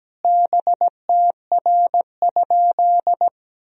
Morse code, heard as a keyed tone, about 17 wpm, 700 Hz. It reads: BTR?
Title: Sentences from Top 400 Words and Abbreviations 17wpm